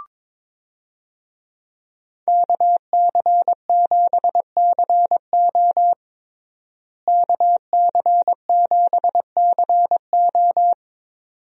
Morse code, heard as a keyed tone, about 21 wpm, 700 Hz. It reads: KC7CO KC7CO
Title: USA Call Signs - 5 Characters - Repeated 2x - 22wpm